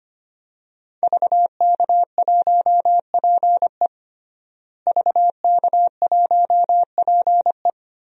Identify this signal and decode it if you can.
Morse code, heard as a keyed tone, about 25 wpm, 700 Hz. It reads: VK1PE VK1PE
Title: DXCC-100 Call Signs - 5 Characters - Repeated 3x - 25wpm